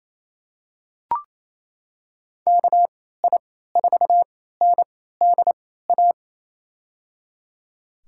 Morse code, heard as a keyed tone, about 28 wpm, spaced as for 16 wpm, 700 Hz. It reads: KI4NDA
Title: USA Call Signs - 3x Character Spacing - 28wpm